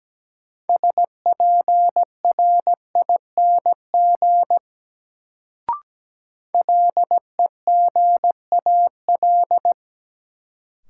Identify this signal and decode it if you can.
Morse code, heard as a keyed tone, about 17 wpm, 700 Hz. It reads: SPRING LEGAL
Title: Sentences from Top 1200 Words - New Word Drill Alt - 17wpm